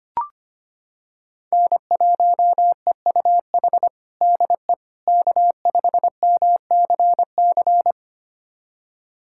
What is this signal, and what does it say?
Morse code, heard as a keyed tone, about 25 wpm, 700 Hz. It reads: N1EUH DE K5MCC
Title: USA Callsign DE Callsign - 5 Characters - 25wpm